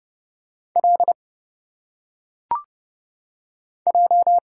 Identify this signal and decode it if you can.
Morse code, heard as a keyed tone, about 30 wpm, 700 Hz. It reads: L J
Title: Single Letters - Phonetic Pronunciation 30wpm